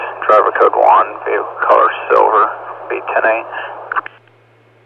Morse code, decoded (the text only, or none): none